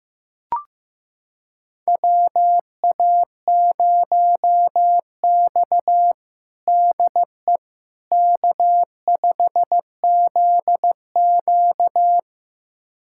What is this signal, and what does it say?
Morse code, heard as a keyed tone, about 15 wpm, 700 Hz. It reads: WA0X DE K5ZQ
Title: USA Callsign DE Callsign - 4 Characters - 15wpm